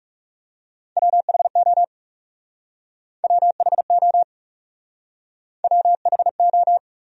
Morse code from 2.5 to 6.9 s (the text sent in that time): WHO WHO